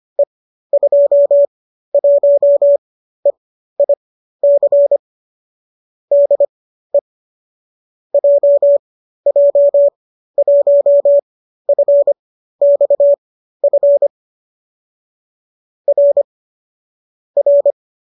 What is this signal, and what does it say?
Morse code, heard as a keyed tone, about 25 wpm, spaced as for 13 wpm, 570 Hz. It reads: E21EIC DE JJ1FXF R R